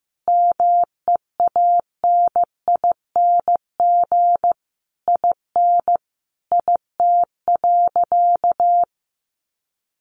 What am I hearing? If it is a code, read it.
Morse code, heard as a keyed tone, about 15 wpm, 700 Hz. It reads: MEANING IN IT.
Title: Sentences from Top 1200 Words 15wpm